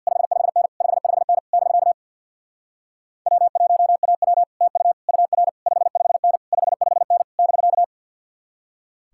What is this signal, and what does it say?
Morse code, heard as a keyed tone, about 63 wpm, 700 Hz. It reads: 55N 55N <BK> W1AW TU UR 55N 55N <BK>